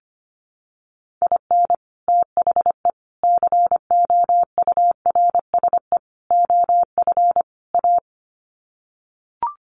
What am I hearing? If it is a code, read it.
Morse code, heard as a keyed tone, about 25 wpm, 700 Hz. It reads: IN THE COURSE OF A
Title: Sets of 5 Words Encore2 25wpm